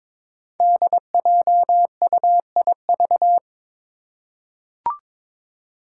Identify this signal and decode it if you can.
Morse code, heard as a keyed tone, about 22 wpm, 700 Hz. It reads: DJUIV
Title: Five Challenging Letter-Number Combinations 22wpm